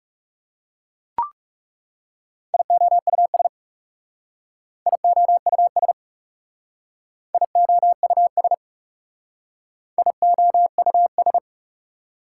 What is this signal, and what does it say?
Morse code, heard as a keyed tone, about 39 wpm, 700 Hz. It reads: IOUS IOUS IOUS IOUS